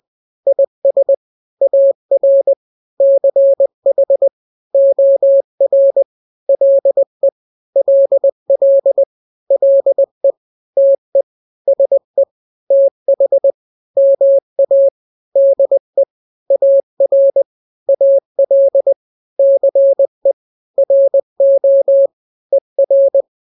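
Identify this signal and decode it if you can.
Morse code, heard as a keyed tone, about 20 wpm, 550 Hz. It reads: IS AR CH OR LE LL LE TE SE TH MA DE AR AL CE RO ER
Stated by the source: Morse drill of two-letter n-grams at 20/18 WPM (Farnsworth), from KK7RR.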